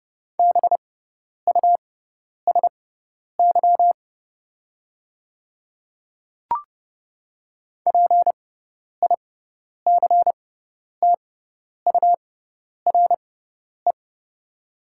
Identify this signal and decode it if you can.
Morse code, heard as a keyed tone, about 30 wpm, spaced as for 10 wpm, 700 Hz. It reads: BUSY PICTURE
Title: 6x Character Spacing - Top 500 Words - 30wpm